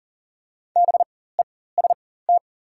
Morse code, heard as a keyed tone, about 37 wpm, spaced as for 18 wpm, 700 Hz. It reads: BEST
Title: 4x Character Spacing - Top 500 Words - 40wpm